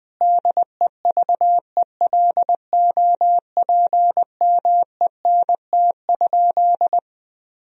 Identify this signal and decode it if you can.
Morse code, heard as a keyed tone, about 20 wpm, 700 Hz. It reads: DEVELOPMENT?
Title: Sentences from Top 1000 Words - Review Part 4 - 20wpm